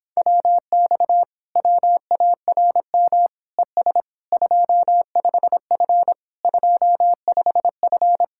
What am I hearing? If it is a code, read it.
Morse code, heard as a keyed tone, about 26 wpm, 700 Hz. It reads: WX WARM ES 25F 25F